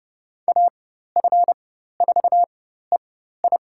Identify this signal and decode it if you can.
Morse code, heard as a keyed tone, about 30 wpm, spaced as for 14 wpm, 700 Hz. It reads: AF4EI
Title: USA Call Signs - 5 Characters - 4x Character Spacing - 30wpm